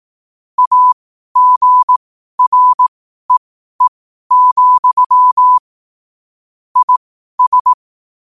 Morse code, heard as a keyed tone, about 18 wpm, spaced as for 12 wpm, 1000 Hz. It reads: AGREE, IS
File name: qotdfiles / 12wpm